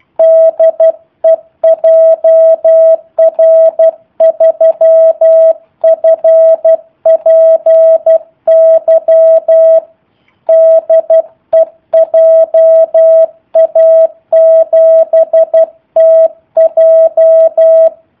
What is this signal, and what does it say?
Morse code, heard as a keyed tone, about 12 wpm, 650 Hz. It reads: DE JR3FPY DE JA7TJ
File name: jh8cbh / 2019cw / sp03- w